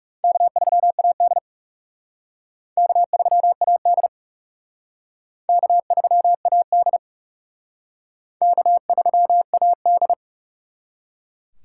Morse code, heard as a keyed tone, about 40 wpm, 700 Hz. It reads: K3AD K3AD K3AD K3AD